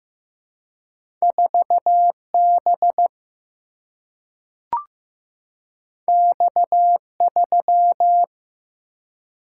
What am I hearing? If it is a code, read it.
Morse code, heard as a keyed tone, about 15 wpm, 700 Hz. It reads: 4 B X 3
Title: Single-Letter Single-Number 15wpm